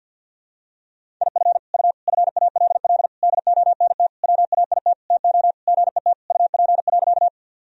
Morse code, heard as a keyed tone, about 52 wpm, 700 Hz. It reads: IF U FALL DONT WAIT TO GET UP.
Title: Sentences from Top 500 Words and Abbreviations 50wpm